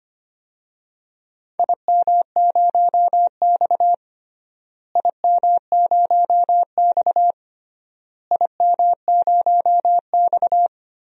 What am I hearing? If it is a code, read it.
Morse code, heard as a keyed tone, about 25 wpm, 700 Hz. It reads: IM0X IM0X IM0X
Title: DXCC-100 Call Signs - 4 Characters - Repeated 3x - 25wpm